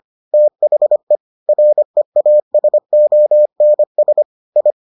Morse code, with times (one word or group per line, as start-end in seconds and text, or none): 0.3-1.2 THE
1.5-4.2 REASONS
4.6-4.7 I